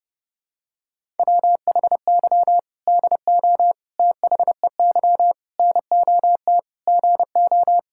Morse code, heard as a keyed tone, about 30 wpm, 700 Hz. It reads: WHY DO THEY NOT GO
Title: Sentences from Top 1600 Words 30wpm